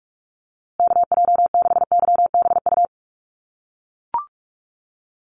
Morse code, heard as a keyed tone, about 44 wpm, 690 Hz. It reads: KJ6YBU
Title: USA Call Signs Repeated 2x 45wpm